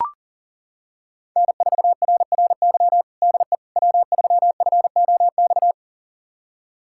Morse code, heard as a keyed tone, about 40 wpm, 700 Hz. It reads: N4RRY DE W3FOX
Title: USA Callsign DE Callsign - 5 Characters - 40wpm